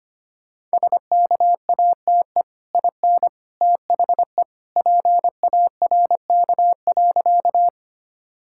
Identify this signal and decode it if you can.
Morse code, heard as a keyed tone, about 25 wpm, 700 Hz. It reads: SKATE IN THE PARK.
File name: Easy Sentences with Hints - 25wpm